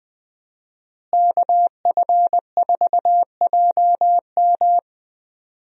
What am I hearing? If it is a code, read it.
Morse code, heard as a keyed tone, about 20 wpm, 700 Hz. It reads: KF4JM
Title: USA Call Signs - 5 Characters - Repeated 3x - 20wpm